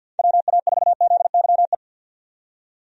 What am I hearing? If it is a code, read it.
Morse code, heard as a keyed tone, about 46 wpm, 700 Hz. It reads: WA4ZYE
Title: USA Call Signs - 6 Characters - 50wpm